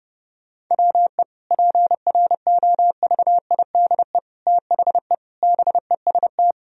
Morse code, heard as a keyed tone, about 29 wpm, 700 Hz. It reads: WE PROVIDE THE BEST
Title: Sentences from Top 1000 Words - Review Part 4 - 30wpm